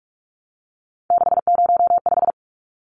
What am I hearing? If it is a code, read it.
Morse code, heard as a keyed tone, about 44 wpm, 700 Hz. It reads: 605
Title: Numbers - 3-Digit 45wpm